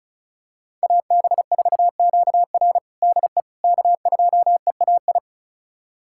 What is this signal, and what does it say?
Morse code, heard as a keyed tone, about 36 wpm, 700 Hz. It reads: AB4QR DE K2EAI
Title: USA Callsign DE Callsign - 5 Characters - 35wpm